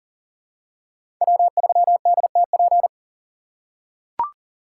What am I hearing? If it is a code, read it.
Morse code, heard as a keyed tone, about 40 wpm, 700 Hz. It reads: W3DTP
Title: USA Call Signs - 5 Characters - 8x Character Spacing - 40wpm